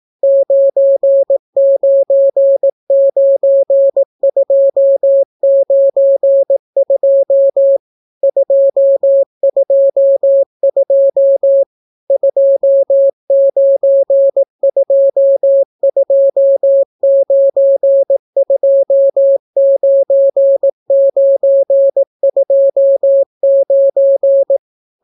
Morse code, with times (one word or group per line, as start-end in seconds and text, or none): 0.2-7.8 999292
8.2-11.6 222
12.1-24.6 2922929929